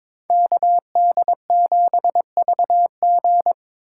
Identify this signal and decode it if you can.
Morse code, heard as a keyed tone, about 26 wpm, 700 Hz. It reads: KD7VG
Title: Speed Racing - USA 2x2 Call Signs 22wpm